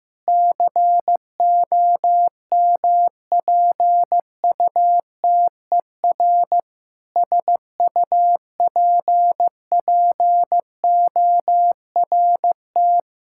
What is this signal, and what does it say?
Morse code, heard as a keyed tone, about 15 wpm, 700 Hz. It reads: COMPUTER SUPPORT
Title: Practice QSO - Exchange 3 - More Info - 15wpm